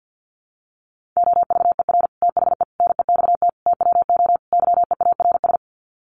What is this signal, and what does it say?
Morse code, heard as a keyed tone, about 50 wpm, 700 Hz. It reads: OVER THE NEXT TWO YEARS
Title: Sets of 5 Words Encore2 50wpm